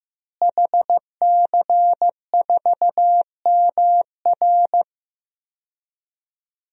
Morse code, heard as a keyed tone, about 15 wpm, 700 Hz. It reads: HC4MR
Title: DXCC-100 Call Signs - 5 Characters - Repeated 2x - 15wpm